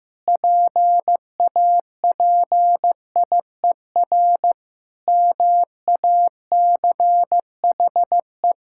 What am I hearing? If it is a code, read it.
Morse code, heard as a keyed tone, about 16 wpm, 700 Hz. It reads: PAPIER MACHE
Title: Pangram Challenge v2 - 15wpm